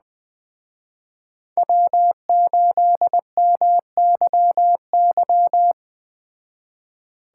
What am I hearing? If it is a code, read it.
Morse code, heard as a keyed tone, about 19 wpm, 700 Hz. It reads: W8MYY
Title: Speed Racing - USA 1x3 Call Signs 17wpm